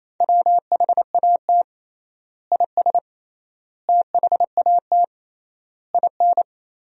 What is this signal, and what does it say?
Morse code, heard as a keyed tone, about 28 wpm, 700 Hz. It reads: WHAT IS THAT IN
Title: 3x Word-Spacing - Sentences from Top 200 Words - 28wpm